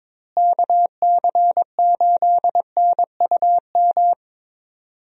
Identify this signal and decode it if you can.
Morse code, heard as a keyed tone, about 21 wpm, 700 Hz. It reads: KC8NUM
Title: USA Call Signs - 6 Characters - Repeated 3x - 22wpm